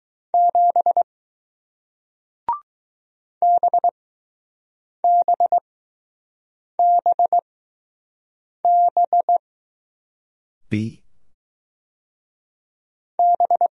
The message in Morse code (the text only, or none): 7 BBBB B